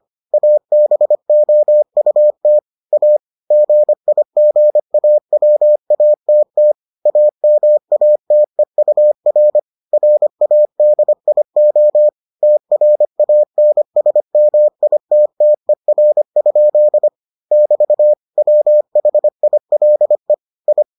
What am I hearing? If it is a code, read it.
Morse code, heard as a keyed tone, about 25 wpm, 600 Hz. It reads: ABOUT A GIGAWATT AMATEUR RADIO TRANSMITTER? <BT> WHILE I